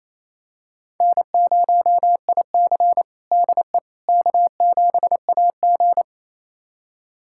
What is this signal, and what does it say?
Morse code, heard as a keyed tone, about 28 wpm, 700 Hz. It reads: N0IC DE K7AG
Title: USA Callsign DE Callsign - 4 Characters - 28wpm